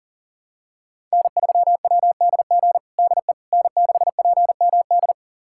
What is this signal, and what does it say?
Morse code, heard as a keyed tone, about 37 wpm, 700 Hz. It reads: N3WDG DE N6PMD
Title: USA Callsign DE Callsign - 5 Characters - 40wpm